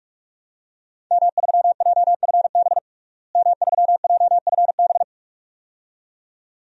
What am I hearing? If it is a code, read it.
Morse code, heard as a keyed tone, about 42 wpm, 700 Hz. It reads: M3JFB M3JFB
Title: DXCC-100 Call Signs - 5 Characters - Repeated 2x - 45wpm